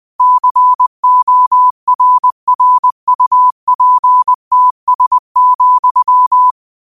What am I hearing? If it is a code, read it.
Morse code, heard as a keyed tone, about 20 wpm, 1000 Hz. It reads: CORRUPTS,
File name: qotdfiles / 20wpm